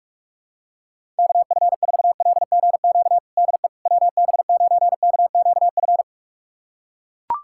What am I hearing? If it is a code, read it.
Morse code, heard as a keyed tone, about 43 wpm, 700 Hz. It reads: KF4LGQ DE WB9KQF
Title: USA Callsign DE Callsign - 6 Characters - 45wpm